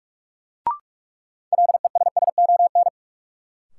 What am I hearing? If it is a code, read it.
Morse code, heard as a keyed tone, about 46 wpm, 700 Hz. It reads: LESSON
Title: Sentences from Top 1200 Words - New Word Drill Alt - 45wpm